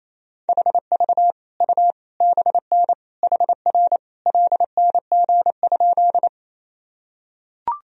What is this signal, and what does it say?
Morse code, heard as a keyed tone, about 27 wpm, 700 Hz. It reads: HV U BN HR LNG?
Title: Sentences from Top 200 Words and Abbreviations 28wpm